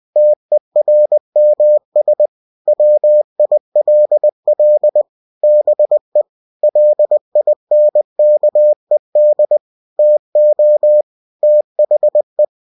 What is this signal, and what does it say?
Morse code, heard as a keyed tone, about 20 wpm, 600 Hz. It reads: TERMS WILL BE LINKED TO THE
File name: cw-books / winnie-the-pooh / winnie-20wpm-page-0115